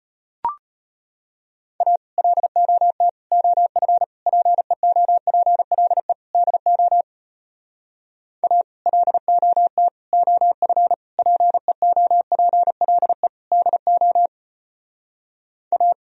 Morse code, heard as a keyed tone, about 33 wpm, 700 Hz. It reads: A LOT OF PEOPLE DO A LOT OF PEOPLE DO A